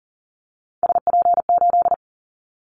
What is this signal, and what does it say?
Morse code, heard as a keyed tone, about 40 wpm, 700 Hz. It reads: SP8
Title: Callsign Prefixes for Top 25 DXCC - Repeated 3x - 40wpm